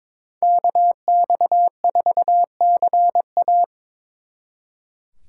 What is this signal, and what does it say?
Morse code, heard as a keyed tone, about 24 wpm, 700 Hz. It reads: KX4CA
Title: Speed Racing - USA 2x2 Call Signs 22wpm